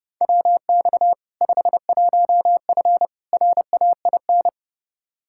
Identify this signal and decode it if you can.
Morse code, heard as a keyed tone, about 30 wpm, 700 Hz. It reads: WX 51F RAIN
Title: K1USN SST QSO Practice 30wpm